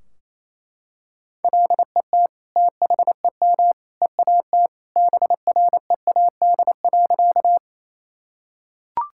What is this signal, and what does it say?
Morse code, heard as a keyed tone, about 28 wpm, 700 Hz. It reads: LET THEM EAT BREAD.
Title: Sentences from Top 1600 Words 28wpm